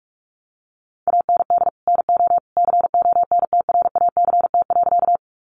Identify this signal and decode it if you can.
Morse code, heard as a keyed tone, about 44 wpm, 700 Hz. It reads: AND NO CONTRACT.
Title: Sentences from Top 1400 Words 45wpm